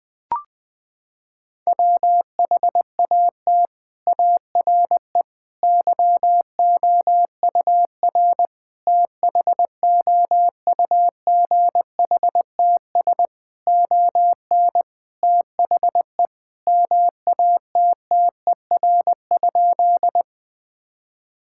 Morse code, heard as a keyed tone, about 20 wpm, 700 Hz. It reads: WHAT ARE YOUR THOUGHTS ON THE MATTER?
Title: Sentences with 6-Word Phrases - 20wpm